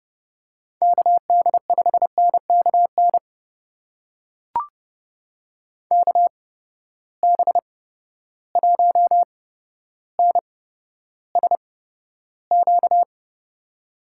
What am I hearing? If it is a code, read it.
Morse code, heard as a keyed tone, about 30 wpm, spaced as for 13 wpm, 700 Hz. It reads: KD5NKN KB1NSQ
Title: USA Call Signs - 6 Characters - 8x Character Spacing - 30wpm